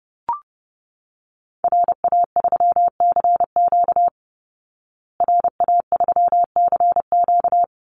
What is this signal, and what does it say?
Morse code, heard as a keyed tone, about 30 wpm, 700 Hz. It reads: RA3CQ RA3CQ
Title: Call Signs Repeated 2x - 30wpm